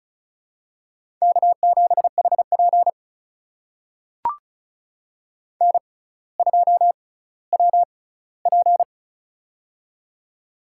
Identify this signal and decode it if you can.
Morse code, heard as a keyed tone, about 34 wpm, spaced as for 18 wpm, 700 Hz. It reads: K7HP N2WP
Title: USA Call Signs - 4 Characters - 6x Character Spacing - 35wpm